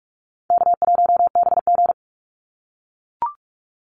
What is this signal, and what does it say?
Morse code, heard as a keyed tone, about 44 wpm, 700 Hz. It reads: K1BG
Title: Easy Call Signs - 45wpm